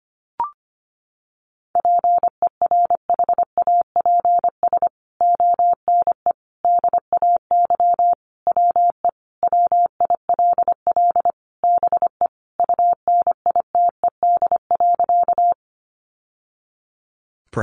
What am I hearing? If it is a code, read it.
Morse code, heard as a keyed tone, about 25 wpm, 700 Hz. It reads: PERHAPS ONE DAY WE WILL BE UNITED.
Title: Sentences from Top 600 Words 25wpm